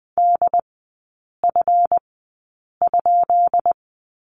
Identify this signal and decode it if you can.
Morse code, heard as a keyed tone, about 20 wpm, spaced as for 8 wpm, 700 Hz. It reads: DF?
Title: 138 Lesson 23e - Letter f - Three Characters 20wpm